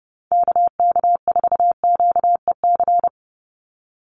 Morse code, heard as a keyed tone, about 30 wpm, 700 Hz. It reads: KK4QEC
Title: USA Call Signs 30wpm